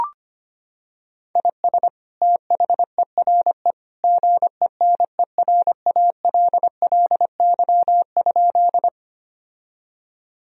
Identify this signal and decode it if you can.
Morse code, heard as a keyed tone, about 25 wpm, 700 Hz. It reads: IS THERE GENERALLY?
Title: Sentences from Top 1000 Words - Review Part 4 - 25wpm